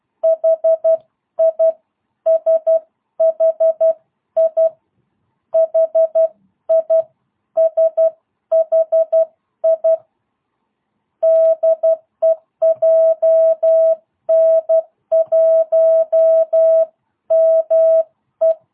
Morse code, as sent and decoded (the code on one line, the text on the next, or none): .... .. / ... .... .. / .... .. / ... .... .. / -.. .--- -. .---- -- .
HI SHI HI SHI DEJN1ME